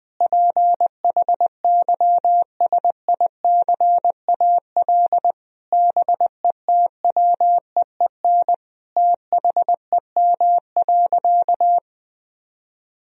Morse code, heard as a keyed tone, about 19 wpm, 700 Hz. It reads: PHYSICAL BETWEEN THEM.
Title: Sentences from Top 1000 Words - Review Part 4 - 20wpm